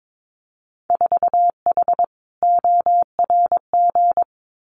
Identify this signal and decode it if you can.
Morse code, heard as a keyed tone, about 22 wpm, 700 Hz. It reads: 4H ORG
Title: Winter Field Day Exchanges Encore 22wpm